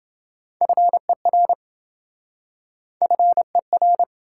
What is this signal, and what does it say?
Morse code, heard as a keyed tone, about 28 wpm, 700 Hz. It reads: FER FER